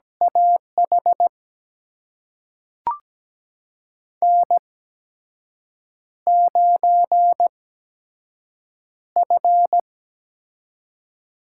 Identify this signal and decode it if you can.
Morse code, heard as a keyed tone, about 17 wpm, spaced as for 6 wpm, 700 Hz. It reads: AH N9F
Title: USA Call Signs - 5 Characters - 8x Character Spacing - 17wpm